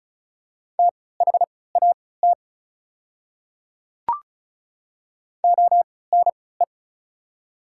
Morse code, heard as a keyed tone, about 36 wpm, spaced as for 20 wpm, 700 Hz. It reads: THAT ONE
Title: 3x Character Spacing - Top 200 Words - 35wpm